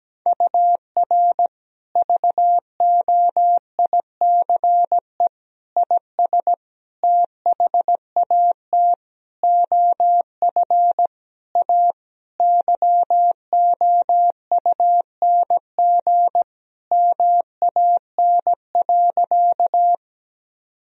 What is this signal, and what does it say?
Morse code, heard as a keyed tone, about 17 wpm, 700 Hz. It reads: UR VOICE IS THAT OF A YOUNG MAN.
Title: Sentences from Top 500 Words and Abbreviations 17wpm